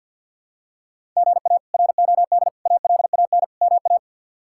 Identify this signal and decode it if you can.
Morse code, heard as a keyed tone, about 48 wpm, 700 Hz. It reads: GA ROD ALAN MA
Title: K1USN SST QSO Practice - 50wpm